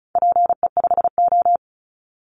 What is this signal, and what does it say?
Morse code, heard as a keyed tone, about 34 wpm, 700 Hz. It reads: PE5O
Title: Call Signs Repeated 3x - 35wpm